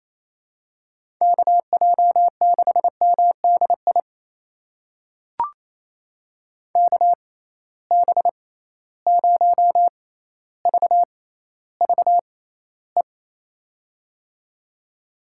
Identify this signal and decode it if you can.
Morse code, heard as a keyed tone, about 28 wpm, spaced as for 14 wpm, 700 Hz. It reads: KJ6MDI KB0VVE
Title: USA Call Signs - 6 Characters - 6x Character Spacing - 28wpm